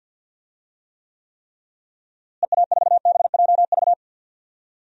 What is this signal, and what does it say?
Morse code, heard as a keyed tone, about 53 wpm, 700 Hz. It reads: EA4BJV
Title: DXCC-100 Call Signs - 6 Characters - Repeated 3x - 50wpm